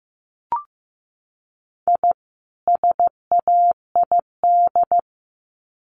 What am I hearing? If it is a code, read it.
Morse code, heard as a keyed tone, about 15 wpm, 700 Hz. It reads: I SAID
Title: Sets of 2 Words 15wpm